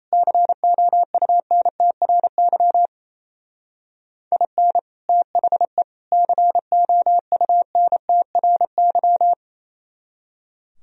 Morse code, heard as a keyed tone, about 35 wpm, 700 Hz. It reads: COUNTRY IN THE COUNTRY